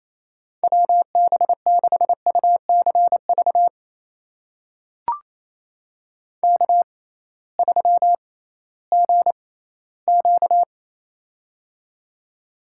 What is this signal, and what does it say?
Morse code, heard as a keyed tone, about 28 wpm, spaced as for 16 wpm, 700 Hz. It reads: WB6UCV K3GQ
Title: USA Call Signs - 6x Character Spacing - 28wpm